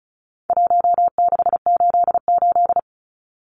Morse code, heard as a keyed tone, about 34 wpm, 700 Hz. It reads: 1688
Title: Numbers - Contest Style - 4-Digit 35wpm